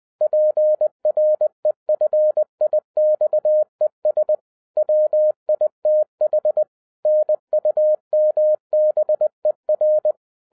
Morse code, heard as a keyed tone, about 20 wpm, 610 Hz. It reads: PREFIXES WITH NUMBER
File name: 2012-01-mp3 / n15-INVESTIGATION INTO THE POSSIBILITY OF WINNING PREFIX VERSION 3 AWARD-20wpm